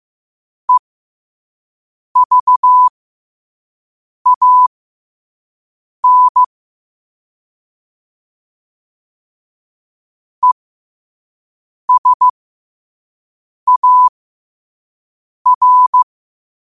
Morse code, heard as a keyed tone, about 15 wpm, spaced as for 5 wpm, 1000 Hz. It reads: EVAN ESAR
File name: qotdfiles / 5wpm